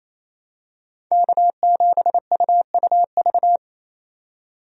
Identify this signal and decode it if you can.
Morse code, heard as a keyed tone, about 28 wpm, 700 Hz. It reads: K7UUV
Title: USA Call Signs - 5 Characters - 28wpm